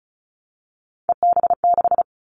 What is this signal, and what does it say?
Morse code, heard as a keyed tone, about 36 wpm, 700 Hz. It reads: EB6
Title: Callsign Prefixes for Top 100 DXCC - 35wpm